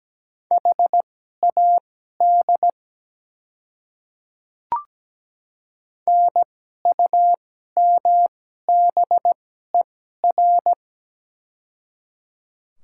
Morse code, heard as a keyed tone, about 17 wpm, spaced as for 12 wpm, 700 Hz. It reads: HAD NUMBER
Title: 2x Character Spacing - Top 100 Words - 17wpm